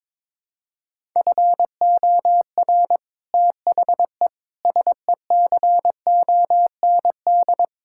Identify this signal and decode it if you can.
Morse code, heard as a keyed tone, about 22 wpm, 700 Hz. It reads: FOR THE SECOND
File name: practice / 22wpm / Sets of 7 Words - 22wpm